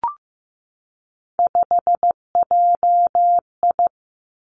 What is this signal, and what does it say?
Morse code, heard as a keyed tone, about 15 wpm, 700 Hz. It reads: 5JI
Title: Three Challenging Letter-Number Combinations 15wpm